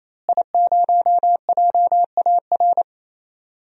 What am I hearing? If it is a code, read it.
Morse code, heard as a keyed tone, about 28 wpm, 690 Hz. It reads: I0JAR
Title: DXCC-100 Call Signs - 5 Characters - Repeated 3x - 28wpm